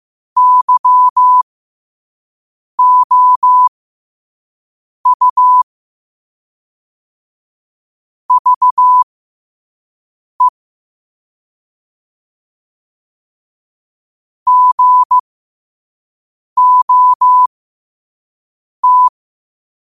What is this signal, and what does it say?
Morse code, heard as a keyed tone, about 15 wpm, spaced as for 5 wpm, 1000 Hz. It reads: YOU VE GOT
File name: qotdfiles / 5wpm